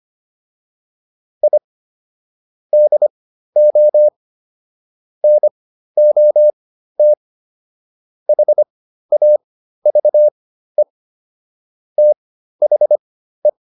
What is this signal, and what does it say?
Morse code, heard as a keyed tone, about 25 wpm, spaced as for 13 wpm, 600 Hz. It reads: I DO NOT HAVE THE